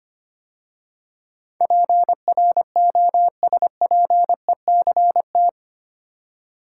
Morse code, heard as a keyed tone, about 25 wpm, 700 Hz. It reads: PROSPECT
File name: 8 Letter Words - First 3 Letters Given - 25wpm